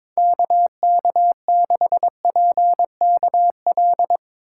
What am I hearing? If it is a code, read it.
Morse code, heard as a keyed tone, about 22 wpm, 700 Hz. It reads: KK6PKL
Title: USA Call Signs - 3x Character Spacing - 22wpm